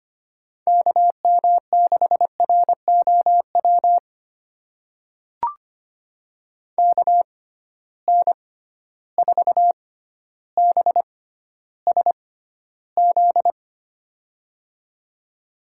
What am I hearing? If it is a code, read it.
Morse code, heard as a keyed tone, about 25 wpm, spaced as for 13 wpm, 700 Hz. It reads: KM6ROW KN4BSZ
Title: USA Call Signs - 6 Characters - 6x Character Spacing - 25wpm